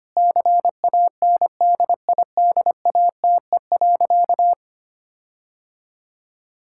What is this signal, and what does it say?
Morse code, heard as a keyed tone, about 25 wpm, 700 Hz. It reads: CANDIDATE.
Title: Sentences from Top 1600 Words 25wpm